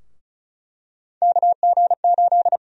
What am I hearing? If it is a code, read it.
Morse code, heard as a keyed tone, about 36 wpm, 700 Hz. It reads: KG8
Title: USA Call Sign Prefixes - 35wpm